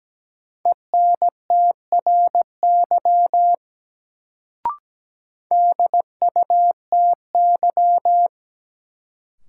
Morse code, heard as a keyed tone, about 17 wpm, 700 Hz. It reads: ENTRY DUTY
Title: Sentences from Top 1500 Words - New Word Drill - 17wpm